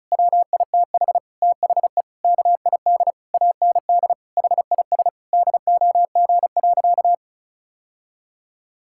Morse code, heard as a keyed tone, about 35 wpm, 700 Hz. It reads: WITH THE KID AND HIS DOG.
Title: Sentences from Top 1600 Words 35wpm